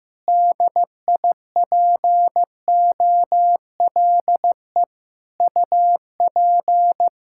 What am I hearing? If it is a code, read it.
Morse code, heard as a keyed tone, about 15 wpm, 700 Hz. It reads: DIPOLE UP
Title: Practice QSO - Exchange 2 - Info - 15wpm